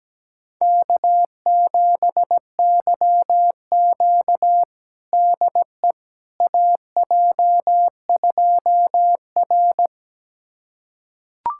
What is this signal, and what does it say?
Morse code, heard as a keyed tone, about 17 wpm, 700 Hz. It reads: K7YQ DE AJ2R
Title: USA Callsign DE Callsign - 4 Characters - 17wpm